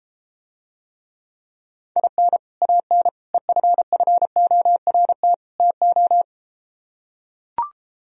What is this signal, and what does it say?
Morse code, heard as a keyed tone, about 33 wpm, 700 Hz. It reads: IN AN EFFORT TO